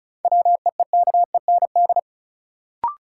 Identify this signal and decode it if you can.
Morse code, heard as a keyed tone, about 36 wpm, 700 Hz. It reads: WEEKEND
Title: Sets of 5 Words - New Word Drill - 35wpm